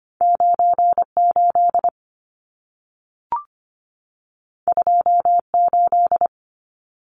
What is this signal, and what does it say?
Morse code, heard as a keyed tone, about 25 wpm, 700 Hz. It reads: 98 28
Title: Numbers - 2-Digit 25wpm